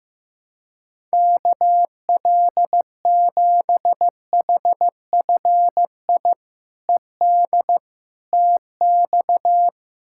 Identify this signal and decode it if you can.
Morse code, heard as a keyed tone, about 15 wpm, 700 Hz. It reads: KL7HFI ED TX
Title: K1USN SST QSO Practice - 15wpm